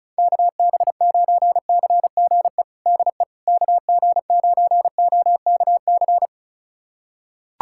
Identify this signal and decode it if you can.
Morse code, heard as a keyed tone, about 34 wpm, 700 Hz. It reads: KB9CGE DE KG9OKC
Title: USA Callsign DE Callsign - 6 Characters - 35wpm